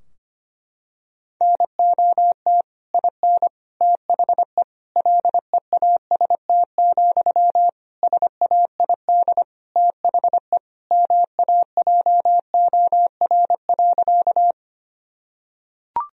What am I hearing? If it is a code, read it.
Morse code, heard as a keyed tone, about 25 wpm, 700 Hz. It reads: NOT IN THE LEAST, SAID THE MAJOR.
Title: Sentences from Top 1000 Words - Review Part 4 - 25wpm